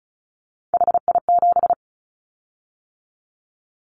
Morse code, heard as a keyed tone, about 34 wpm, 700 Hz. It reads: HI7
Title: Callsign Prefixes for Top 100 DXCC - 35wpm